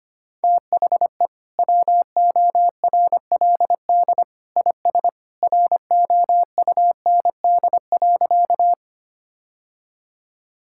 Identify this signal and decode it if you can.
Morse code, heard as a keyed tone, about 25 wpm, 700 Hz. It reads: THE WORLD IS ROUND.
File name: practice / 25wpm / Sentences with 2-Word Phrases - 25wpm